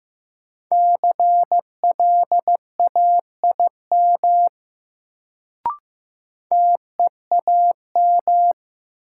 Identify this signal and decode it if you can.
Morse code, heard as a keyed tone, about 15 wpm, 700 Hz. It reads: CLAIM TEAM
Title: Sentences from Top 1500 Words - New Word Drill - 15wpm